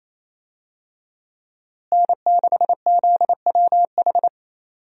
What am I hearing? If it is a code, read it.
Morse code, heard as a keyed tone, about 27 wpm, 700 Hz. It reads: N6ZWH
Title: USA Call Signs - 5 Characters - Repeated 2x - 28wpm